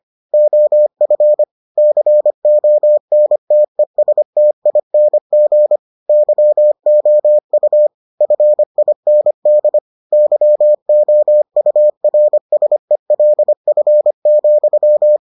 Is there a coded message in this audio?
Yes: Morse code from 0.3 to 15.2 s, heard as a keyed tone, about 25 wpm, 600 Hz. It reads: OF CONTESTING YOU FIND YOURSELF,